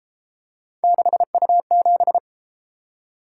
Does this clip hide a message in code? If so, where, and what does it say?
Yes: Morse code from 0.8 to 2.2 s, heard as a keyed tone, about 31 wpm, 700 Hz. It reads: 6U7